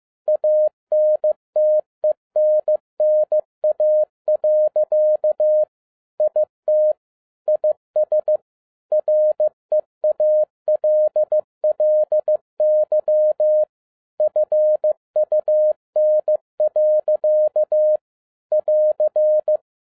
Morse code, heard as a keyed tone, about 15 wpm, 610 Hz. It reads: ANTENNA. IT IS REALLY FUN. <AR>
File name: 2012-01-mp3 / n07-ARA JR0DIL-15wpm